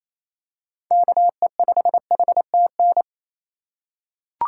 Morse code, heard as a keyed tone, about 27 wpm, 700 Hz. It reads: KE5HTN
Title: USA Call Signs - 6 Characters - 4x Character Spacing - 28wpm